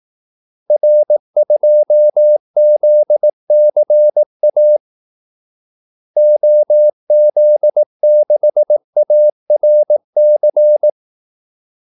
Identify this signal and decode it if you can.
Morse code, heard as a keyed tone, about 18 wpm, 600 Hz. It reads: R2ZCA OZ6ARC
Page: Kaldesignaler 18 wpm